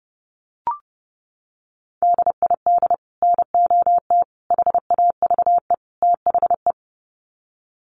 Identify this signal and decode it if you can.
Morse code, heard as a keyed tone, about 30 wpm, 700 Hz. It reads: DID NOT HAVE THE
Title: Sets of 4 Words 30wpm